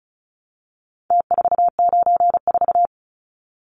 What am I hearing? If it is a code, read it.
Morse code, heard as a keyed tone, about 34 wpm, 700 Hz. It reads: T494
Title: Numbers - Contest Style - 4-Digit 35wpm